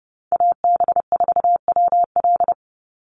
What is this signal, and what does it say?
Morse code, heard as a keyed tone, about 30 wpm, 700 Hz. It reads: AB4WL
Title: NA SOTA Chaser Callsigns 30wpm